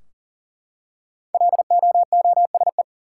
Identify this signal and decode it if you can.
Morse code, heard as a keyed tone, about 40 wpm, 700 Hz. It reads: LOOSE
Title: Five Letter Words - 40wpm